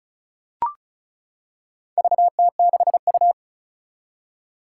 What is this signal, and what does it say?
Morse code, heard as a keyed tone, about 34 wpm, 700 Hz. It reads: VT6U
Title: Four Challenging Letter-Number Combinations 35wpm